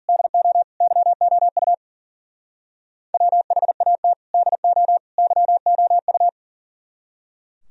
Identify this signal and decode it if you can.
Morse code, heard as a keyed tone, about 48 wpm, 700 Hz. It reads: DO YOU WHAT DO YOU
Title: Speed-Racing - Sets of 3 Words 40wpm